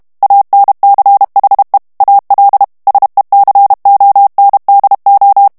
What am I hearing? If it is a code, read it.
Morse code, heard as a keyed tone, about 32 wpm, 800 Hz. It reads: ANCHE AL SECONDO
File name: ITALIEN texte clair 2 vitesse 30 mots minute